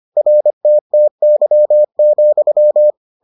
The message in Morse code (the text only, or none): RTTY,